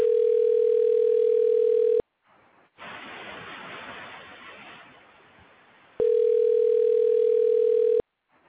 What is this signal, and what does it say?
Ringback tone (North America), 440 + 480 Hz.